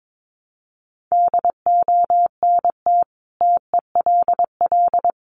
Morse code, heard as a keyed tone, about 22 wpm, 700 Hz. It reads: DONT TELL
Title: Sentences from Top 700 Words 22wpm